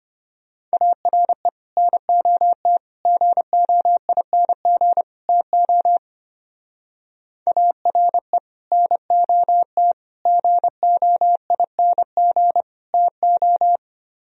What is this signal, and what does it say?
Morse code, heard as a keyed tone, about 32 wpm, 700 Hz. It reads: ARE NOT GOING TO ARE NOT GOING TO